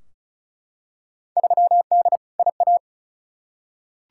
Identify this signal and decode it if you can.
Morse code, heard as a keyed tone, about 36 wpm, 690 Hz. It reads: 3D IA
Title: Field Day Exchanges Encore - 35wpm